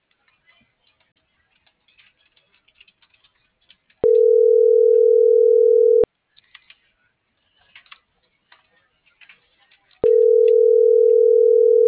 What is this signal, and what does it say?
Ringback tone (North America), 440 + 480 Hz.